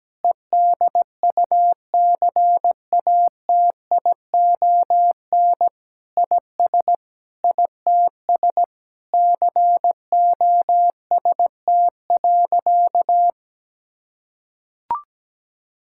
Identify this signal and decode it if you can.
Morse code, heard as a keyed tone, about 17 wpm, 700 Hz. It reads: EDUCATION IS ITS COST.
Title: Sentences from Top 1000 Words - Review Part 6 - 17wpm